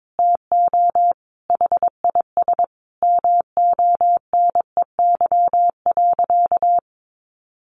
Morse code, heard as a keyed tone, about 22 wpm, 700 Hz. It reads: TO HIS MONEY.
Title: Sentences from Top 1200 Words 22wpm